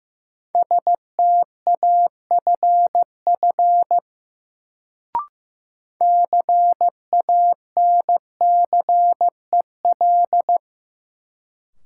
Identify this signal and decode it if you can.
Morse code, heard as a keyed tone, about 15 wpm, 700 Hz. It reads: STAFF CANCEL
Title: Sentences from Top 1200 Words - New Word Drill Alt - 15wpm